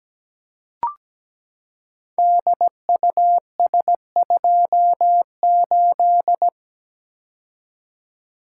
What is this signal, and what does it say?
Morse code, heard as a keyed tone, about 17 wpm, 700 Hz. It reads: DUS28
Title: Code Groups - 5 Letter:Numbers - 17wpm